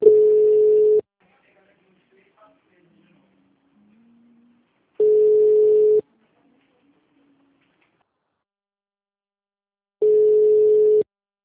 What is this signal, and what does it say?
Ringback tone (Europe, 425 Hz).